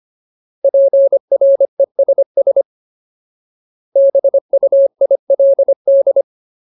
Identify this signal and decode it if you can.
Morse code, heard as a keyed tone, about 25 wpm, 550 Hz. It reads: PRESS BUILD
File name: Ultimate Word List / 25WPM Ultimate